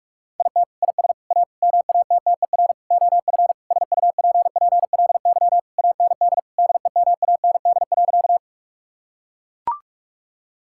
Morse code, heard as a keyed tone, about 41 wpm, 700 Hz. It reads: IT IS A MATTER OF SUPPLY AND DEMAND.
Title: Sentences from Top 1000 Words - Review Part 5 - 45wpm